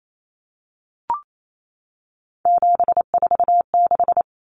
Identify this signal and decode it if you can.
Morse code, heard as a keyed tone, about 28 wpm, 700 Hz. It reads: 746
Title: Common Cut Numbers - 3-Digit 28wpm